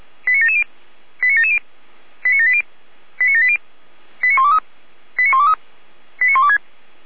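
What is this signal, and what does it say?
Five-tone selective call (ZVEI), heard as quick runs of five tones, tones of about 80 ms. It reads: 89770 89770 89780 89780 89123 89123 89126
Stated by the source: ZVEI five-tone selective call 89760, received by SDR.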